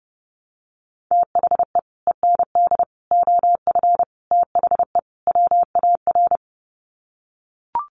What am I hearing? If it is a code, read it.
Morse code, heard as a keyed tone, about 29 wpm, 700 Hz. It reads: THE END OF THE WAR
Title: Sets of 5 Words 30wpm